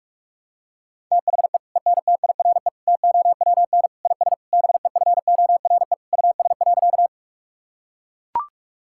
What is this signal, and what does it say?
Morse code, heard as a keyed tone, about 43 wpm, 700 Hz. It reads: THE ENTIRE TOWN IS BEFORE US.
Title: Sentences from Top 1000 Words - Review Part 5 - 45wpm